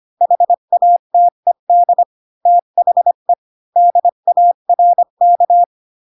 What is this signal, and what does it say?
Morse code, heard as a keyed tone, about 25 wpm, 700 Hz. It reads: HATED THE DARK
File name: Pinocchio / Pinocchio ch6 20 25 ik1hge